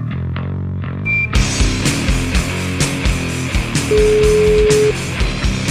Ringback tone (Europe, 425 Hz), with noise.